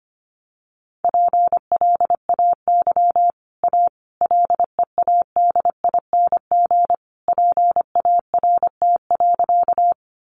Morse code, heard as a keyed tone, about 25 wpm, 700 Hz. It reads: PLAY A LEADING PART.
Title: Sentences from Top 1100 Words 25wpm